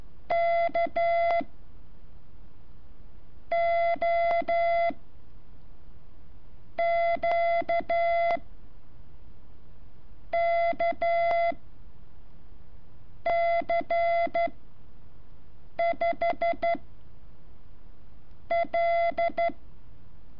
Morse code, heard as a keyed tone, about 12 wpm, spaced as for 5 wpm, 680 Hz. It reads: K O Q K C5 L
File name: curso / ejercicio 16